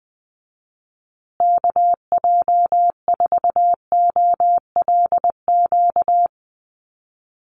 Morse code, heard as a keyed tone, about 20 wpm, 700 Hz. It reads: KJ4OLQ
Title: USA Call Signs Repeated 2x 20wpm